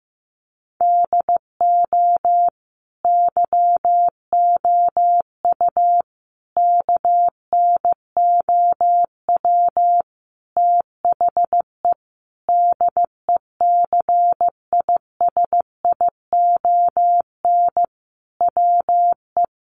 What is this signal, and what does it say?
Morse code, heard as a keyed tone, about 15 wpm, 700 Hz. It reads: DO YOU KNOW THE DECISION WE
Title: Sentences from Top 1000 Words 15wpm